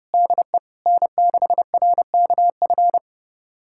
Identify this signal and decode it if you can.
Morse code, heard as a keyed tone, about 30 wpm, 700 Hz. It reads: DE N6RKF
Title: USA Callsign DE Callsign - 5 Characters - 30wpm